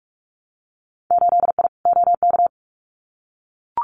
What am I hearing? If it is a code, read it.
Morse code, heard as a keyed tone, about 44 wpm, 700 Hz. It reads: 8I OK
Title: Winter Field Day Exchanges Encore 45wpm